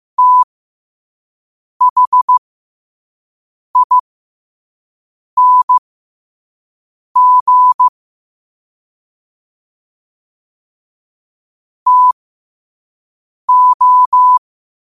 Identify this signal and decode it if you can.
Morse code, heard as a keyed tone, about 15 wpm, spaced as for 5 wpm, 1000 Hz. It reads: THING TO
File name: qotdfiles / 5wpm